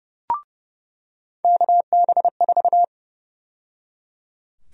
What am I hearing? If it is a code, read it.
Morse code, heard as a keyed tone, about 30 wpm, 700 Hz. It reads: KB4
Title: USA Call Sign Prefixes - 30wpm